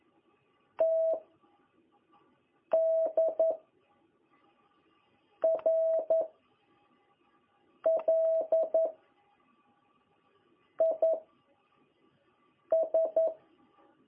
Morse code, heard as a keyed tone, about 11 wpm, 660 Hz. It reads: T D R L I S
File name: jh8cbh / 2021cw